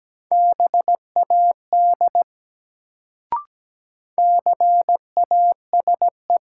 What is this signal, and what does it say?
Morse code, heard as a keyed tone, about 17 wpm, 700 Hz. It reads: BAD CASE
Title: Sentences from Top 700 Words - New Word Drill - 17wpm